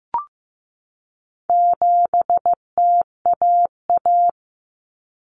Morse code, heard as a keyed tone, about 15 wpm, 700 Hz. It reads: 7TAA
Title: Common Cut Numbers - 4-Digit 15wpm